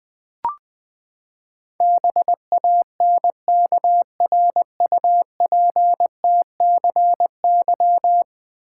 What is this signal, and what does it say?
Morse code, heard as a keyed tone, about 20 wpm, 700 Hz. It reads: BANKRUPTCY
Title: Ten Letter Words - 20wpm